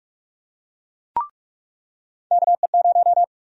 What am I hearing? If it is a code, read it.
Morse code, heard as a keyed tone, about 41 wpm, 700 Hz. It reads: KE0
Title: USA Call Sign Prefixes - 45wpm